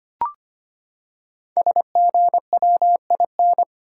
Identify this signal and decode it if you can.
Morse code, heard as a keyed tone, about 25 wpm, 700 Hz. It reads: SGWIN
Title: Code Groups - 5 Letters - 25wpm